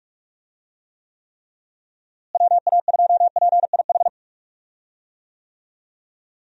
Morse code, heard as a keyed tone, about 47 wpm, 700 Hz. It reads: WA2PIH
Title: USA Call Signs - 6 Characters - Repeated 3x - 45wpm